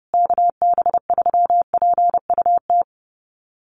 Morse code, heard as a keyed tone, about 30 wpm, 700 Hz. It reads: KB3PUT
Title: USA Call Signs Repeated 3x 30wpm